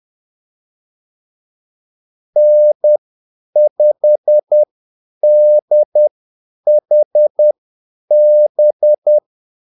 Morse code, heard as a keyed tone, about 10 wpm, spaced as for 8 wpm, 600 Hz. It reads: N5DHB
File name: Callsigns-10-8wpm